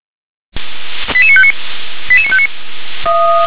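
Five-tone selective call (ZVEI), heard as quick runs of five tones, tones of about 70 ms. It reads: 88959 88959
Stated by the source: ZVEI five-tone selective call received by SDR.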